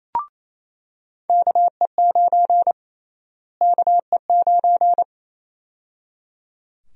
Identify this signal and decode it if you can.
Morse code, heard as a keyed tone, about 29 wpm, 700 Hz. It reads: KE9 KE9
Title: USA Call Sign Prefixes - Repeated 2x - 28wpm